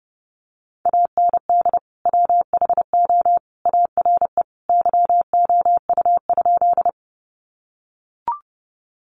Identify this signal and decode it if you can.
Morse code, heard as a keyed tone, about 30 wpm, 700 Hz. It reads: AND WHO ARE YOU?
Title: Sentences from Top 100 Words 30wpm